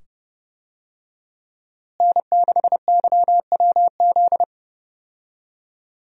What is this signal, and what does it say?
Morse code, heard as a keyed tone, about 30 wpm, 700 Hz. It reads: N6YWZ
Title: USA Call Signs - 5 Characters - Repeated 2x - 30wpm